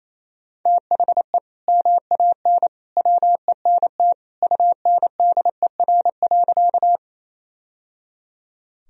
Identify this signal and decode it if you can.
Morse code, heard as a keyed tone, about 28 wpm, 700 Hz. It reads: THE MAN WENT UNDER.
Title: Sentences from Top 1000 Words - Review Part 1 - 28wpm